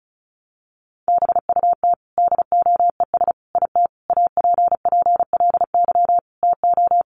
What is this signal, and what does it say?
Morse code, heard as a keyed tone, about 34 wpm, 700 Hz. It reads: BUT DOES IT APPLY TO
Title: Sentences from Top 1200 Words 35wpm